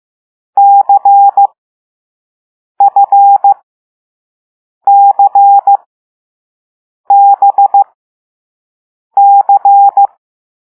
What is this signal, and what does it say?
Morse code, heard as a keyed tone, about 15 wpm, spaced as for 5 wpm, 800 Hz. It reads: CFCBC
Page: Play Lesson 6 C - all letters in previous lessons